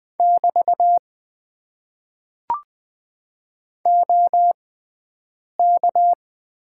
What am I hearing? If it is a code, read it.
Morse code, heard as a keyed tone, about 21 wpm, spaced as for 7 wpm, 700 Hz. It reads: <BT> OK
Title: Top 90 QSO Elements - 6x Character Spacing - 20wpm